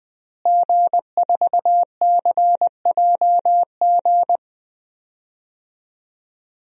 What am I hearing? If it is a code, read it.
Morse code, heard as a keyed tone, about 20 wpm, 700 Hz. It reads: G4CJG
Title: DXCC-100 Call Signs - 5 Characters - Repeated 2x - 20wpm